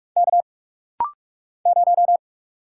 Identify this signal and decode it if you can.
Morse code, heard as a keyed tone, about 46 wpm, 700 Hz. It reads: K 0
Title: Single Letter-Number - Rapid-Fire 45wpm